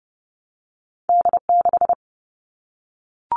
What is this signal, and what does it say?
Morse code, heard as a keyed tone, about 31 wpm, 700 Hz. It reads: D6
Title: All Cut Numbers - 2-Digit 30wpm